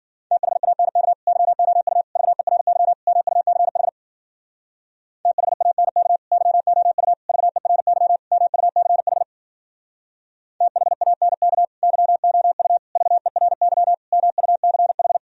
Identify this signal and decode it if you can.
Morse code, heard as a keyed tone, about 52 wpm, 700 Hz. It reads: THANK YOU VERY MUCH THANK YOU VERY MUCH THANK YOU VERY MUCH